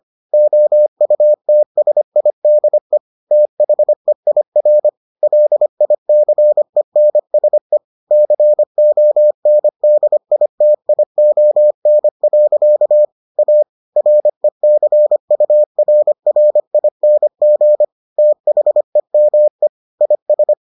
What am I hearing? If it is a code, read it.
Morse code, heard as a keyed tone, about 25 wpm, 600 Hz. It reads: OUTSIDE THEIR LICENSE CONDITION. A RECURRING THEME IS